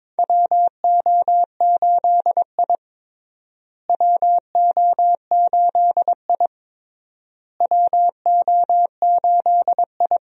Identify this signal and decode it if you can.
Morse code, heard as a keyed tone, about 22 wpm, 700 Hz. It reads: WO8I WO8I WO8I
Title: USA Call Signs - 4 Characters - Repeated 3x - 22wpm